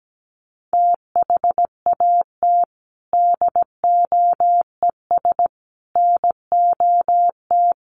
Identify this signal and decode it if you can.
Morse code, heard as a keyed tone, about 17 wpm, 700 Hz. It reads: THAT DOES NOT
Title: Sets of 4 Words 17wpm